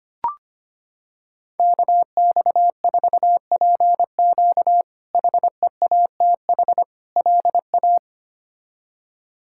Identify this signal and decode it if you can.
Morse code, heard as a keyed tone, about 25 wpm, 700 Hz. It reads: KX4PQ HEATH LA
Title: K1USN SST QSO Practice - 25wpm